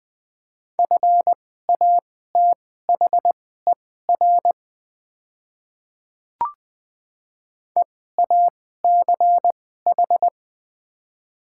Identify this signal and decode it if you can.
Morse code, heard as a keyed tone, about 21 wpm, spaced as for 15 wpm, 700 Hz. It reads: FATHER EACH
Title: 2x Character Spacing - Top 200 Words - 20wpm